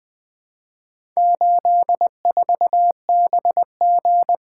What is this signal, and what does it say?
Morse code, heard as a keyed tone, about 20 wpm, 700 Hz. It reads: 84BG
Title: Four Letter-Number Combinations - 20wpm